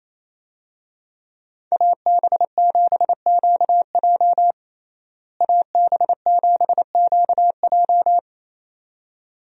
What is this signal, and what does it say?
Morse code, heard as a keyed tone, about 28 wpm, 700 Hz. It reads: AB7QJ AB7QJ
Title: USA Call Signs - 5 Characters - Repeated 2x - 28wpm